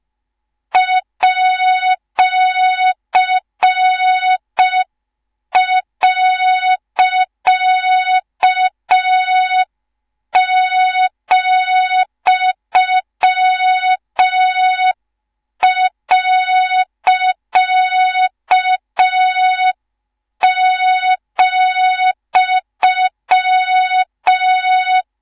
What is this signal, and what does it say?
Morse code, heard as a keyed tone, about 5 wpm, 3700 Hz, with noise. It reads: @.,.,